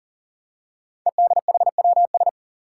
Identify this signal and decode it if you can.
Morse code, heard as a keyed tone, about 42 wpm, 700 Hz. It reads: EDHWS
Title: Five Challenging Letter-Number Combinations 40wpm